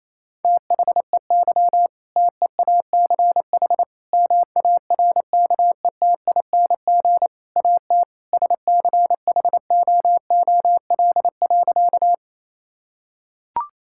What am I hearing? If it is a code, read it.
Morse code, heard as a keyed tone, about 28 wpm, 700 Hz. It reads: THEY TEACH MARKETING AT SCHOOL.
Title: Sentences from Top 1000 Words - Review Part 5 - 28wpm